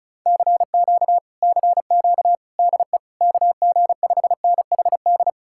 Morse code, heard as a keyed tone, about 47 wpm, spaced as for 41 wpm, 700 Hz. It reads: CQ CQ DE KG5NHD
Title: Speed-Racing Common QSO Phrases - 35wpm